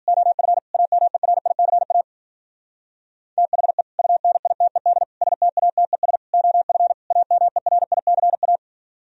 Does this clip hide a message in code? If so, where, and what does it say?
Yes: Morse code from 0.1 to 8.6 s, heard as a keyed tone, about 50 wpm, 700 Hz. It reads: OF AMERICA THE UNITED STATES OF AMERICA